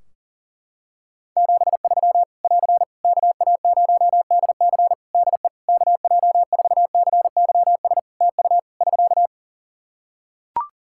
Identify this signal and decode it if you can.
Morse code, heard as a keyed tone, about 40 wpm, 700 Hz. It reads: 73 <AR> KA0DC DE KJ4CYS TU <SK>
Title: K1USN SST QSO Practice 40wpm